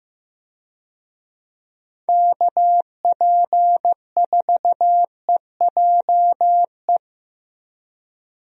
Morse code, heard as a keyed tone, about 15 wpm, 700 Hz. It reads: KP4EJE
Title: DXCC-100 Call Signs - 6 Characters - Repeated 3x - 15wpm